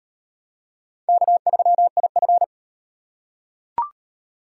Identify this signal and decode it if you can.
Morse code, heard as a keyed tone, about 36 wpm, 700 Hz. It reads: K3IF